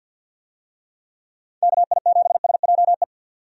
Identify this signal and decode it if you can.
Morse code, heard as a keyed tone, about 51 wpm, 700 Hz. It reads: KI7SJE
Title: USA Call Signs - 6 Characters - 50wpm